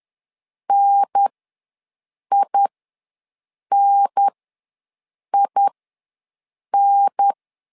Morse code, heard as a keyed tone, about 10 wpm, spaced as for 6 wpm, 800 Hz. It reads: NININ
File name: LEZIONI / 3 MWRO / 3 MW ws 10wpm 2213